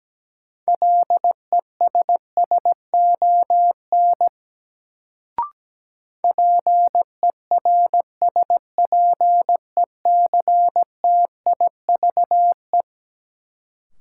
Morse code, heard as a keyed tone, about 17 wpm, 700 Hz. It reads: LESSON PERSPECTIVE
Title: Sentences from Top 1200 Words - New Word Drill - 17wpm